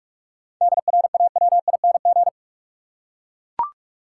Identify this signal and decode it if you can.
Morse code, heard as a keyed tone, about 42 wpm, 700 Hz. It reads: DRAWING
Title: Seven Letter Words - 45wpm